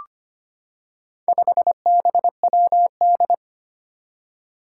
Morse code, heard as a keyed tone, about 25 wpm, 700 Hz. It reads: 5BWD
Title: Four Challenging Letter-Number Combinations 25wpm